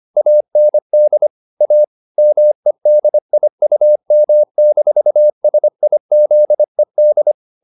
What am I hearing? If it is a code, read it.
Morse code, heard as a keyed tone, about 25 wpm, 600 Hz. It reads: AND A MEDIUM-SIZED
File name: the old man and the sea part 1 25wpm